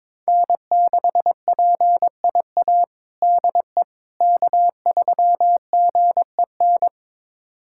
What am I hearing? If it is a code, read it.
Morse code, heard as a keyed tone, about 21 wpm, 700 Hz. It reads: N6PIA DE K3GEN
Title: USA Callsign DE Callsign - 5 Characters - 22wpm